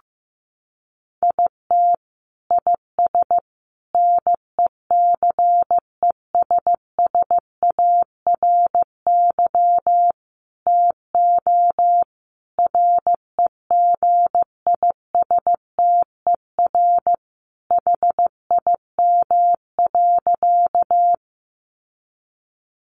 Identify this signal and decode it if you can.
Morse code, heard as a keyed tone, about 14 wpm, 700 Hz. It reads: IT IS NECESSARY TO REGISTER HIM.
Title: Sentences from Top 1200 Words 15wpm